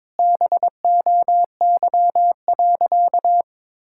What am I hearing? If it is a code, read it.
Morse code, heard as a keyed tone, about 22 wpm, 700 Hz. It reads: BOY.
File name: practice / 22wpm / Sentences with 8-Word Phrases - 22wpm